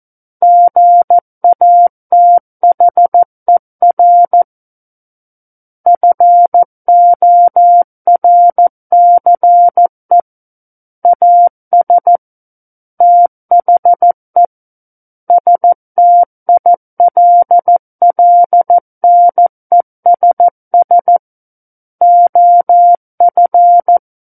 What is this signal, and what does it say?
Morse code, heard as a keyed tone, about 14 wpm, 700 Hz. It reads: GATHER FORCE AS THE STILLNESS OF
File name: Book One / 05-Chapter 5 - 14 WPM